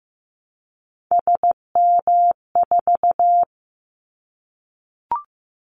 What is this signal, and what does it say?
Morse code, heard as a keyed tone, about 15 wpm, 700 Hz. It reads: SM4
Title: Callsign Prefixes for Top 25 DXCC - Repeated 3x - 15wpm